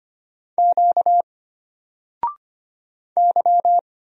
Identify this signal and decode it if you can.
Morse code, heard as a keyed tone, about 25 wpm, 700 Hz. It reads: Q Y
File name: Single-Letter-Number-25wpm